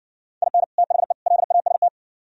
Morse code, heard as a keyed tone, about 60 wpm, 700 Hz. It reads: IN THE LAST